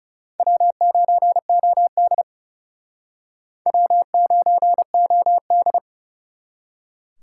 Morse code, heard as a keyed tone, about 38 wpm, 700 Hz. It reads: W9OD W9OD